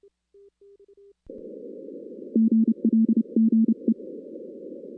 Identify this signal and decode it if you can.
Morse code, heard as a keyed tone, about 31 wpm, 220 Hz. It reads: GLGE
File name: 2016 arrl dx cw / 2016-ARRL-DX-CW-K5TR- - 0-17